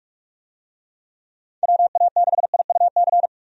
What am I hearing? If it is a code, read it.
Morse code, heard as a keyed tone, about 44 wpm, 700 Hz. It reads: WA6IUC
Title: USA Call Signs - 6 Characters - 45wpm